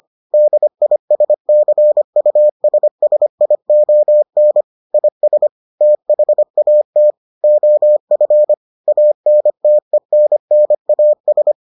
Morse code, heard as a keyed tone, about 25 wpm, 600 Hz. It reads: DISCUSSION IS THAT OF ANTENNAS